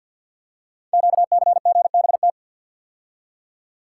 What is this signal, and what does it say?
Morse code, heard as a keyed tone, about 48 wpm, 700 Hz. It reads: QCGBT
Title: Code Groups - 5 Letter:Numbers - 50wpm